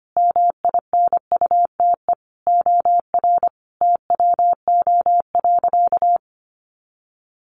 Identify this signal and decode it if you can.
Morse code, heard as a keyed tone, about 25 wpm, 700 Hz. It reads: MINUTE OR TWO.
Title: Sentences from Top 1400 Words 25wpm